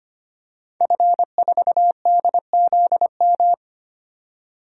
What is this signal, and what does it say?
Morse code, heard as a keyed tone, about 25 wpm, 700 Hz. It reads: F4DZM
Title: DXCC-100 Call Signs - 5 Characters - 25wpm